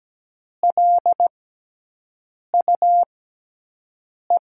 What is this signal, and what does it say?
Morse code, heard as a keyed tone, about 17 wpm, spaced as for 6 wpm, 700 Hz. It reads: LUE
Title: USA Call Signs - 6 Characters - 6x Character Spacing - 17wpm